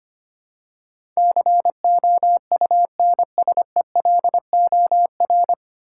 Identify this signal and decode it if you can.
Morse code, heard as a keyed tone, about 25 wpm, 690 Hz. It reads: COUNSELOR
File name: practice / 25wpm / 9 Letter Words - First 3 Letters Given - 25wpm